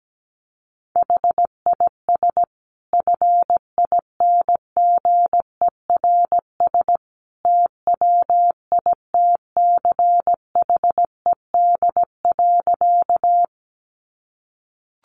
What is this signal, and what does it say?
Morse code, heard as a keyed tone, about 17 wpm, 700 Hz. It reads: HIS FINGERS TWITCHED.
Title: A Sound Of Thunder 17wpm